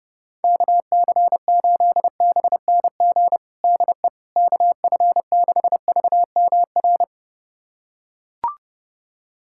Morse code, heard as a keyed tone, about 30 wpm, 700 Hz. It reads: KC8BNG DE KF6VMR
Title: USA Callsign DE Callsign - 6 Characters - 30wpm